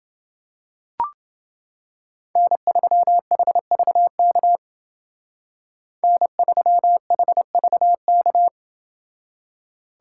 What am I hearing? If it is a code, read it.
Morse code, heard as a keyed tone, about 28 wpm, 700 Hz. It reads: N3HVK N3HVK